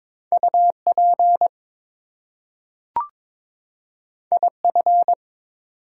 Morse code, heard as a keyed tone, about 22 wpm, 700 Hz. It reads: UP IF
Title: Two Letter Words - 22wpm